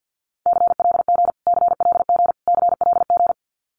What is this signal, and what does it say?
Morse code, heard as a keyed tone, about 50 wpm, 700 Hz. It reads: CLG CLG CLG
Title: Abbreviations Repeated 3x - 50wpm